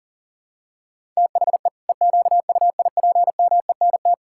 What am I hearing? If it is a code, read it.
Morse code, heard as a keyed tone, about 38 wpm, 700 Hz. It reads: THE EQUIPMENT
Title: Sentences from Top 1000 Words - Review Part 5 - 40wpm